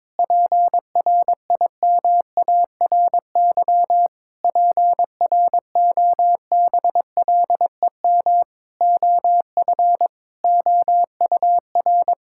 Morse code, heard as a keyed tone, about 22 wpm, 700 Hz. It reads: PRIMARY PROBLEM OF OUR
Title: Sentences from Top 1000 Words - Review Part 5 - 22wpm